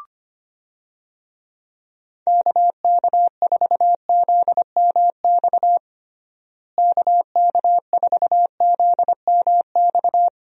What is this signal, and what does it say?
Morse code, heard as a keyed tone, about 25 wpm, 700 Hz. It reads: KK4ZMX KK4ZMX
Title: USA Call Signs - 6 Characters - Repeated 2x - 25wpm